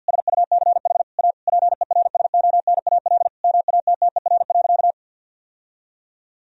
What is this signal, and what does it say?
Morse code, heard as a keyed tone, about 50 wpm, 700 Hz. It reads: SUCH A PERSONAL MATTER.
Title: Sentences from Top 1000 Words - Review Part 4 - 50wpm